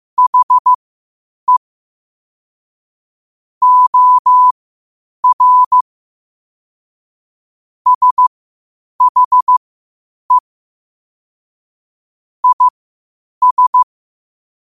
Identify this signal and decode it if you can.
Morse code, heard as a keyed tone, about 15 wpm, spaced as for 8 wpm, 1000 Hz. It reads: HE OR SHE IS
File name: qotdfiles / 8wpm